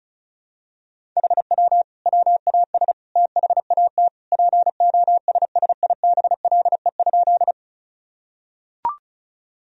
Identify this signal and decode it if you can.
Morse code, heard as a keyed tone, about 36 wpm, 700 Hz. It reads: HW WAS THAT POSSIBLE?
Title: Sentences from Top 500 Words and Abbreviations 35wpm